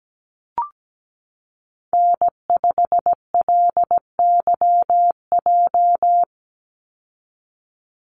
Morse code, heard as a keyed tone, about 17 wpm, 700 Hz. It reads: N5LYJ
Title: Call Signs Repeated 3x - 17wpm